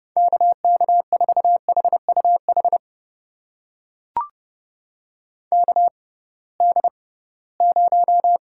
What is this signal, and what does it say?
Morse code, heard as a keyed tone, about 30 wpm, spaced as for 19 wpm, 700 Hz. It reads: KK4HUH KD0
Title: USA Call Signs - 6 Characters - 6x Character Spacing - 30wpm